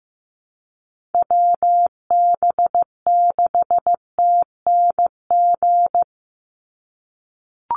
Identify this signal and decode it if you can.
Morse code, heard as a keyed tone, about 15 wpm, 700 Hz. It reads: WB6TNG
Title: USA Call Signs 15wpm